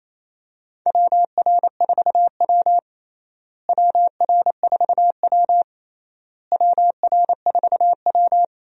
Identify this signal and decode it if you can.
Morse code, heard as a keyed tone, about 28 wpm, 700 Hz. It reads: WR4W WR4W WR4W
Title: USA Call Signs - 4 Characters - Repeated 3x - 28wpm